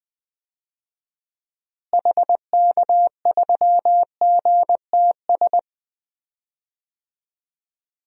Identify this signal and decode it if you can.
Morse code, heard as a keyed tone, about 20 wpm, 700 Hz. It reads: HK3GTS
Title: DXCC-100 Call Signs - 6 Characters - Repeated 3x - 20wpm